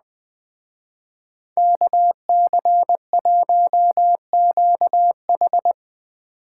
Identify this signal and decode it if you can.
Morse code, heard as a keyed tone, about 20 wpm, 700 Hz. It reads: KC1QH
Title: Speed Racing - USA Call Signs 17wpm